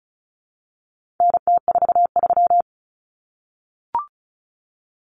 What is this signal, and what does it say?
Morse code, heard as a keyed tone, about 34 wpm, 700 Hz. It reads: NT43
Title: Common Cut Numbers - 4-Digit 35wpm